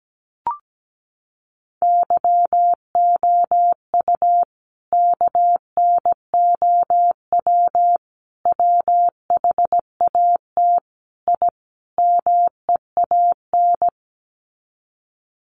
Morse code, heard as a keyed tone, about 17 wpm, 700 Hz. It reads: YOU KNOW WHAT I MEAN
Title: Sets of 5 Words 17wpm